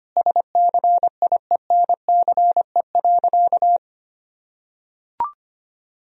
Morse code, heard as a keyed tone, about 25 wpm, 700 Hz. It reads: SCIENCE.
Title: Sentences from Top 1000 Words - Review Part 2 - 25wpm